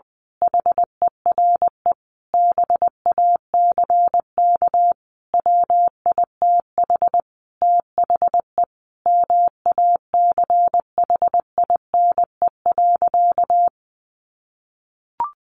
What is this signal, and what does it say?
Morse code, heard as a keyed tone, about 20 wpm, 700 Hz. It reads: HERE BACK WITH THE MACHINE.
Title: A Sound Of Thunder 20wpm